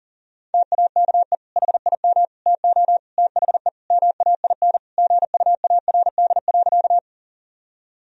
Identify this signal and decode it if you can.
Morse code, heard as a keyed tone, about 40 wpm, 700 Hz. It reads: TAKE HIM TO THE MAIN GUARD.
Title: Sentences from Top 1000 Words - Review Part 4 - 40wpm